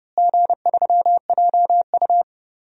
Morse code, heard as a keyed tone, about 30 wpm, 700 Hz. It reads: G3JU
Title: DXCC-100 Call Signs - 4 Characters - Repeated 3x - 30wpm